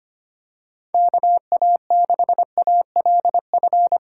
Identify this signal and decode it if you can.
Morse code, heard as a keyed tone, about 28 wpm, 700 Hz. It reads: KA6ALF
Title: Speed Racing - USA 2x3 Call Signs 25wpm